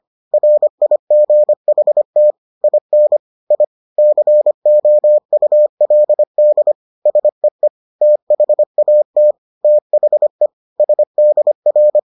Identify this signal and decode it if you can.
Morse code, heard as a keyed tone, about 25 wpm, 600 Hz. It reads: RIGHT IN I COULD SEE THAT THE SDR